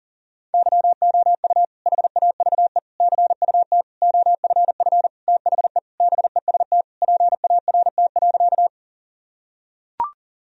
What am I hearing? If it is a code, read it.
Morse code, heard as a keyed tone, about 40 wpm, 700 Hz. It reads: YOU HAVE CUT OFF THE BEST PART.
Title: Sentences from Top 1000 Words - Review Part 1 - 40wpm